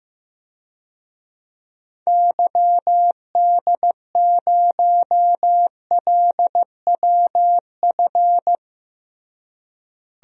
Morse code, heard as a keyed tone, about 15 wpm, 700 Hz. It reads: YD0LWF
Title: DXCC-100 Call Signs - 6 Characters - Repeated 3x - 15wpm